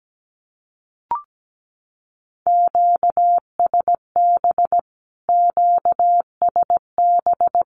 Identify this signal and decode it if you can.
Morse code, heard as a keyed tone, about 17 wpm, 700 Hz. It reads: QSB QSB
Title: Q-Codes Repeated 3x 17wpm